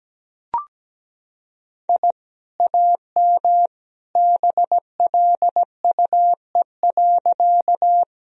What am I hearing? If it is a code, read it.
Morse code, heard as a keyed tone, about 17 wpm, 700 Hz. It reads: I AM BLUE.
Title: Sentences from Top 1000 Words - Review Part 3 - 17wpm